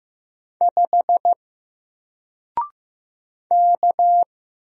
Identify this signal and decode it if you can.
Morse code, heard as a keyed tone, about 15 wpm, 700 Hz. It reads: 5 K
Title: Single Letter-Number Alt 15wpm